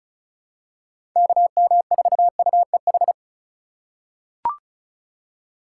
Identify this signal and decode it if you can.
Morse code, heard as a keyed tone, about 36 wpm, 700 Hz. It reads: KM4UEH
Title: USA Call Signs - 4x Character Spacing - 35wpm